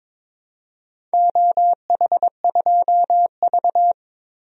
Morse code, heard as a keyed tone, about 22 wpm, 700 Hz. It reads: OH2V
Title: DXCC-100 Call Signs - 4 Characters - 22wpm